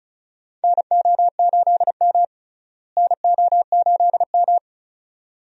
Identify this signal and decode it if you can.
Morse code, heard as a keyed tone, about 34 wpm, 700 Hz. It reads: NO8M NO8M
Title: USA Call Signs - 4 Characters - Repeated 2x - 35wpm